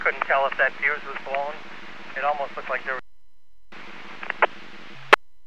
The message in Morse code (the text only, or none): none